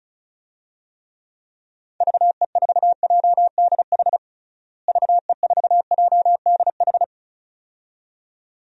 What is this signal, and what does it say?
Morse code, heard as a keyed tone, about 34 wpm, 700 Hz. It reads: VE4JDH VE4JDH
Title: DXCC-100 Call Signs - 6 Characters - Repeated 2x - 35wpm